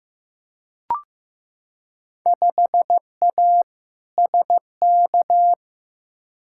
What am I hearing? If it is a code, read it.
Morse code, heard as a keyed tone, about 15 wpm, 700 Hz. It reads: 5A SK
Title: Field Day Exchanges Encore - 15wpm